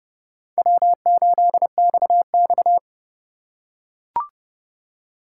Morse code, heard as a keyed tone, about 30 wpm, 700 Hz. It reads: W8XX
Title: USA Call Signs - 4 Characters - 6x Character Spacing - 30wpm